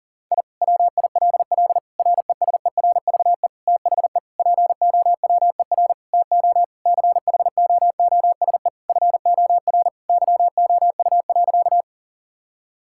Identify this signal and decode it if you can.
Morse code, heard as a keyed tone, about 40 wpm, 700 Hz. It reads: I WILL RESERVE THE POWER TO CHOOSE FOR YOU.
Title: Sentences from Top 1000 Words - Review Part 6 - 40wpm